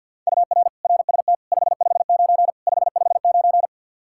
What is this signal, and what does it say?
Morse code, heard as a keyed tone, about 51 wpm, 700 Hz. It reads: UR RST 559 559
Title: Practice QSO - Exchange 1 - Beginning - 50wpm